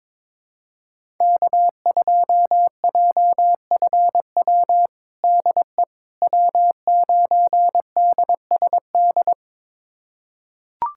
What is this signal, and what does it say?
Morse code, heard as a keyed tone, about 22 wpm, 700 Hz. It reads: K2JFW DE W9DSD
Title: USA Callsign DE Callsign - 5 Characters - 22wpm